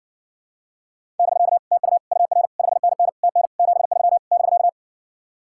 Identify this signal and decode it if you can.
Morse code, heard as a keyed tone, about 60 wpm, 700 Hz. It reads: <BK> TU UR 5NN TN 73 <BK>